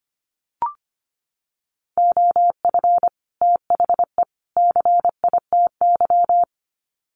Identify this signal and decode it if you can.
Morse code, heard as a keyed tone, about 25 wpm, 700 Hz. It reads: OF THE CITY
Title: Sets of 3 Words 25wpm